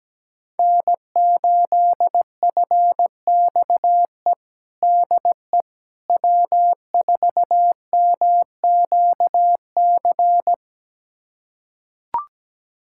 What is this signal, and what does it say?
Morse code, heard as a keyed tone, about 17 wpm, 700 Hz. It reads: N8FXE DE W4MQC
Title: USA Callsign DE Callsign 17wpm